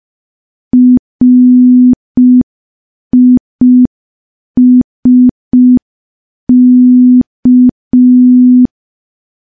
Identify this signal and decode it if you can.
Morse code, heard as a keyed tone, about 5 wpm, 250 Hz. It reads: RISK
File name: morse / morse-training-05WPM